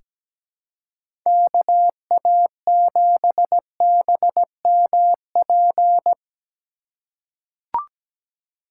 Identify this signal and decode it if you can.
Morse code, heard as a keyed tone, about 17 wpm, 700 Hz. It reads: KA7BMP
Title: USA Call Signs - 6 Characters - 4x Character Spacing - 17wpm